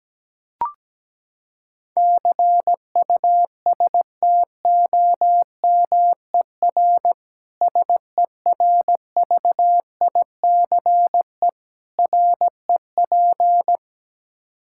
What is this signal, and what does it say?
Morse code, heard as a keyed tone, about 17 wpm, 700 Hz. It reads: CUSTOMER SERVICE REP
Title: Practice QSO - Exchange 3 - More Info - 17wpm